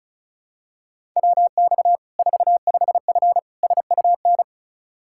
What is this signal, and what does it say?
Morse code, heard as a keyed tone, about 36 wpm, 700 Hz. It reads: WX 45F SUN
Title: K1USN SST QSO Practice 35wpm